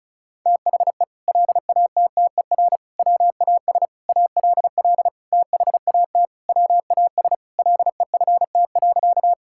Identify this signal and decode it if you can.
Morse code, heard as a keyed tone, about 34 wpm, 700 Hz. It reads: THE LATTER WAS ALL THAT WAS LEFT.
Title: Sentences from Top 1600 Words 35wpm